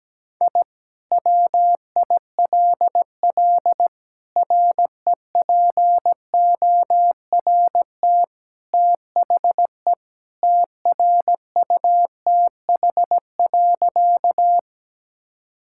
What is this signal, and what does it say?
Morse code, heard as a keyed tone, about 17 wpm, 700 Hz. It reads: I WILL REPORT THE TRUTH.
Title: Sentences from Top 1000 Words - Review Part 6 - 17wpm